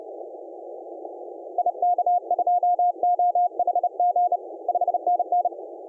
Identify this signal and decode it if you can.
Morse code, heard as a keyed tone, about 31 wpm, 710 Hz. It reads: IK2OHG 5NN
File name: d66d-comoros-on-15m-cw